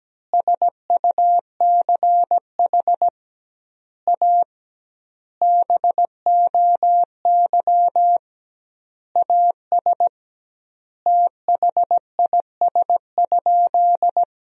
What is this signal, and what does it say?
Morse code, heard as a keyed tone, about 17 wpm, 700 Hz. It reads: SUCH A BOY AS THIS?
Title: 2x Word-Spacing - Sentences from Top 400 Words - 17wpm